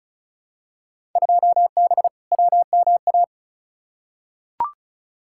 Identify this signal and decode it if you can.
Morse code, heard as a keyed tone, about 34 wpm, 700 Hz. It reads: 2B WMA
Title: Field Day Exchanges Encore - 35wpm